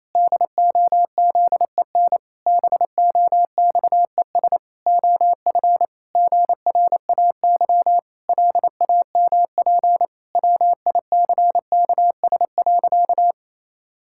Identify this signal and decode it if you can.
Morse code, heard as a keyed tone, about 28 wpm, 700 Hz. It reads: DOZEN BOXES OF GRAY LAMP WICKS.
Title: Pangram Challenge v2 - 28wpm